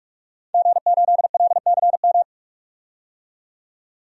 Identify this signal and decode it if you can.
Morse code, heard as a keyed tone, about 46 wpm, 700 Hz. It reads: G8LCM
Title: DXCC-100 Call Signs - 5 Characters - 45wpm